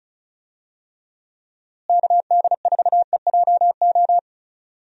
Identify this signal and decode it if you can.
Morse code, heard as a keyed tone, about 36 wpm, 700 Hz. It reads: KD4EJO
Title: USA Call Signs - 6 Characters - Repeated 3x - 35wpm